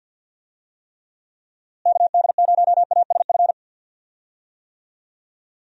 Morse code, heard as a keyed tone, about 50 wpm, 700 Hz. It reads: KD0ASF
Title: USA Call Signs - 6 Characters - Repeated 3x - 50wpm